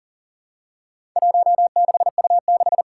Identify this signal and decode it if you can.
Morse code, heard as a keyed tone, about 40 wpm, 700 Hz. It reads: 16U6
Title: Four Challenging Letter-Number Combinations 40wpm